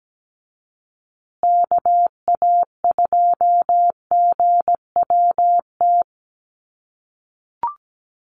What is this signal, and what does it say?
Morse code, heard as a keyed tone, about 17 wpm, 700 Hz. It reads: KA2GWT
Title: USA Call Signs Repeated 2x 17wpm